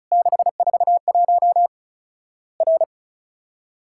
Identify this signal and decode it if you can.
Morse code, heard as a keyed tone, about 34 wpm, 700 Hz. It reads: 641 R
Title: Worked All Europe DX Contest - QTC Exchange - 35wpm